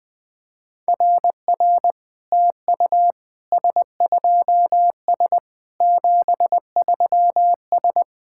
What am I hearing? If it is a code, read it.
Morse code, heard as a keyed tone, about 21 wpm, 700 Hz. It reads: RR TU S2S 73S
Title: SOTA QSO Practice 20wpm